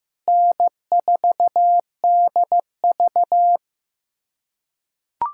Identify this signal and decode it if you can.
Morse code, heard as a keyed tone, about 15 wpm, 700 Hz. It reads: N4DV
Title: USA Callsign DE Callsign - 4 Characters - 15wpm